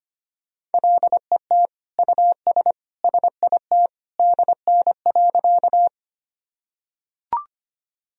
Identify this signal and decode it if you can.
Morse code, heard as a keyed tone, about 25 wpm, 700 Hz. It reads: LET US SIT DN.
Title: Sentences from Top 500 Words and Abbreviations 25wpm